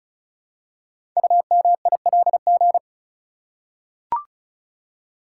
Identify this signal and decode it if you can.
Morse code, heard as a keyed tone, about 36 wpm, 700 Hz. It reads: UMILG
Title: Code Groups - 5 Letters - 35wpm